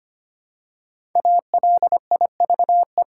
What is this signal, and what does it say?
Morse code, heard as a keyed tone, about 25 wpm, 700 Hz. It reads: ALIVE
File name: 5 Letter Words - First 3 Letters Given - 25wpm